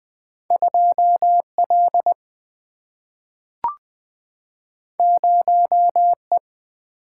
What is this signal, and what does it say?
Morse code, heard as a keyed tone, about 20 wpm, 700 Hz. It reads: 2L 0E
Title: Single-Letter Single-Number 20wpm